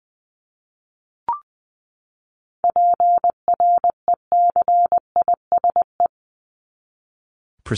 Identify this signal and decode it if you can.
Morse code, heard as a keyed tone, about 20 wpm, 700 Hz. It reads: PRECISE
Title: LY Suffix - Top 100 Words with Base Words 20wpm